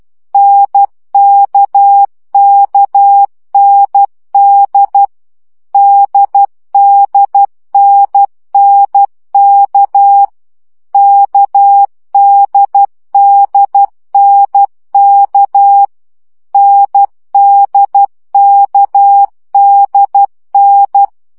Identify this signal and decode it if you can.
Morse code, heard as a keyed tone, about 12 wpm, 800 Hz. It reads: NKKND DDNNK KDDNK NDKDN
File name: CW gyakorlas n d k 10 baud